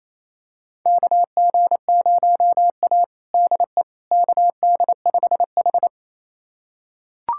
Morse code, heard as a keyed tone, about 27 wpm, 700 Hz. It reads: KG0A DE KD5H
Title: USA Callsign DE Callsign - 4 Characters - 28wpm